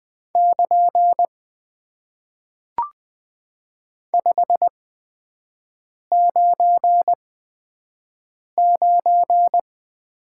Morse code, heard as a keyed tone, about 20 wpm, 700 Hz. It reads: <KN> 5 9 9
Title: Top 90 QSO Elements - 8x Character Spacing - 20wpm